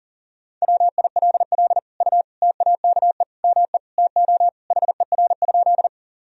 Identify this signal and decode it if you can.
Morse code, heard as a keyed tone, about 37 wpm, 700 Hz. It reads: WILL U TAKE ME TO HER?
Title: Sentences from Top 200 Words and Abbreviations 40wpm